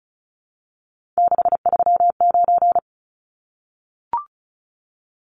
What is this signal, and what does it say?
Morse code, heard as a keyed tone, about 34 wpm, 700 Hz. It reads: 639
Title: Numbers - 3-Digit 35wpm